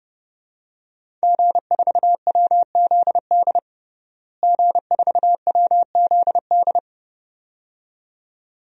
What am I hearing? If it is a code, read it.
Morse code, heard as a keyed tone, about 30 wpm, 700 Hz. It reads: G4WZD G4WZD
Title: DXCC-100 Call Signs - 5 Characters - Repeated 2x - 30wpm